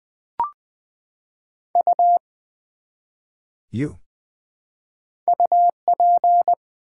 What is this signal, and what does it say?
Morse code, heard as a keyed tone, about 20 wpm, 700 Hz. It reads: U UP
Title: Build A Word - 1 Letter At A Time - 4-Letter Words - 20wpm